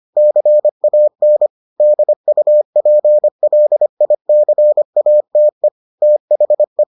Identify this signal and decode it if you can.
Morse code, heard as a keyed tone, about 25 wpm, 600 Hz. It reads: CAN DUPLICATE THE